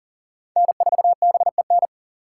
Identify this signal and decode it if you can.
Morse code, heard as a keyed tone, about 39 wpm, 700 Hz. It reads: N4BEN
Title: USA Call Signs - 5 Characters - 2x Character Spacing - 40wpm